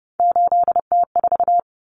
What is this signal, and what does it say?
Morse code, heard as a keyed tone, about 30 wpm, 700 Hz. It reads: 8T4
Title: Common Cut Numbers - 3-Digit 30wpm